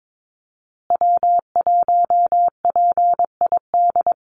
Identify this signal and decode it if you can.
Morse code, heard as a keyed tone, about 22 wpm, 700 Hz. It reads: W1PID
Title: Call Signs Repeated 3x - 22wpm